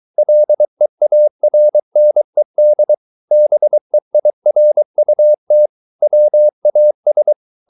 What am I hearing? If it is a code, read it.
Morse code, heard as a keyed tone, about 23 wpm, 600 Hz. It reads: LEARNED BEIRUT WAS